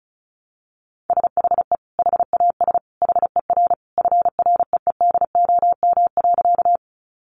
Morse code, heard as a keyed tone, about 34 wpm, 700 Hz. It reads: SHE HAS HER FREEDOM.
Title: Sentences from Top 800 Words 35wpm